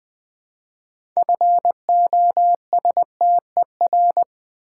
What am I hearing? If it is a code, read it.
Morse code, heard as a keyed tone, about 20 wpm, 700 Hz. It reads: FOSTER
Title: Build A Word - 3 Letters At A Time - 6-Letter Words - 20wpm